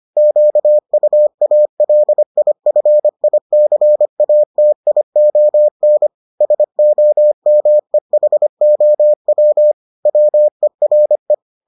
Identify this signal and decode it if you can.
Morse code, heard as a keyed tone, about 25 wpm, 600 Hz. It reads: QUALIFICATION SOMEHOW WERE